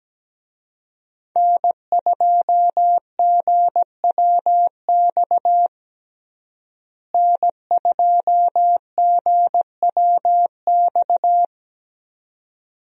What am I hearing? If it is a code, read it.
Morse code, heard as a keyed tone, about 17 wpm, 700 Hz. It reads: N2GWX N2GWX
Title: USA Call Signs - 5 Characters - Repeated 2x - 17wpm